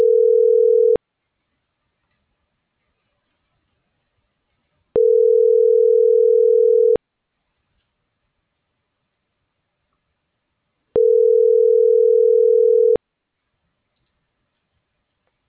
Ringback tone (North America), 440 + 480 Hz.